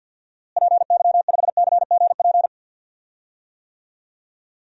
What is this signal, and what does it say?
Morse code, heard as a keyed tone, about 49 wpm, 700 Hz. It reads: PY5CGP
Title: DXCC-100 Call Signs - 6 Characters - Repeated 3x - 50wpm